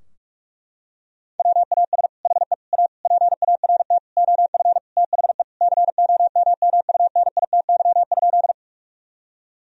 Morse code, heard as a keyed tone, about 46 wpm, 700 Hz. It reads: WAS HE A PART OF THE COMMUNITY?
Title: Sentences from Top 1000 Words - Review Part 4 - 45wpm